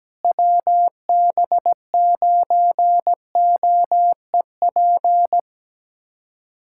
Morse code, heard as a keyed tone, about 17 wpm, 700 Hz. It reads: WB9OEP
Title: USA Callsign DE Callsign - 6 Characters - 17wpm